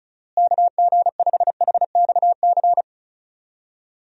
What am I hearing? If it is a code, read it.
Morse code, heard as a keyed tone, about 34 wpm, 700 Hz. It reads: KG5HXC
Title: USA Call Signs - 6 Characters - Repeated 3x - 35wpm